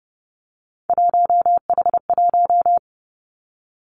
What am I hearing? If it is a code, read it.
Morse code, heard as a keyed tone, about 30 wpm, 700 Hz. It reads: 1H1
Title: Three Challenging Letter-Number Combinations 30wpm